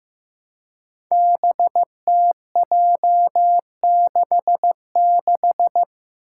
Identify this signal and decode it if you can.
Morse code, heard as a keyed tone, about 15 wpm, 700 Hz. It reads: BTJ66
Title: Five Challenging Letter-Number Combinations 15wpm